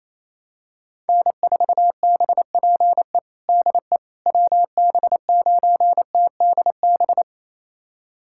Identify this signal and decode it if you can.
Morse code, heard as a keyed tone, about 28 wpm, 700 Hz. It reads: N4BPE DE WB9TDB
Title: USA Callsign DE Callsign 28wpm